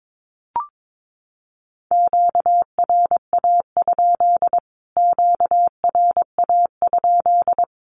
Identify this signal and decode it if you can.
Morse code, heard as a keyed tone, about 22 wpm, 700 Hz. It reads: QRA? QRA?
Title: Q-Codes Repeated 3x 22wpm